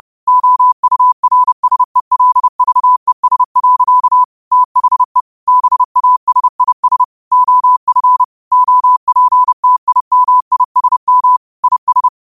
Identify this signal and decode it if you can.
Morse code, heard as a keyed tone, about 30 wpm, 1000 Hz. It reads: OURSELVES. THE BASIS OF OPTIMISM IS
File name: qotdfiles / 30wpm